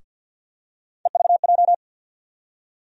Morse code, heard as a keyed tone, about 49 wpm, 700 Hz. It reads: EVJ
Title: USA Call Sign Postfixes - 3 Letters 50wpm